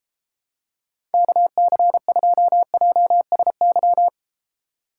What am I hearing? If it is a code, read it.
Morse code, heard as a keyed tone, about 32 wpm, 700 Hz. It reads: KC2JSY